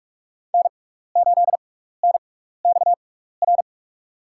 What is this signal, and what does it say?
Morse code, heard as a keyed tone, about 42 wpm, spaced as for 15 wpm, 700 Hz. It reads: N8NXR
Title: USA Call Signs - 5 Characters - 6x Character Spacing - 45wpm